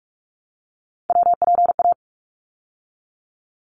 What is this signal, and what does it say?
Morse code, heard as a keyed tone, about 44 wpm, 700 Hz. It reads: WPA
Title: ARRL Sections 45wpm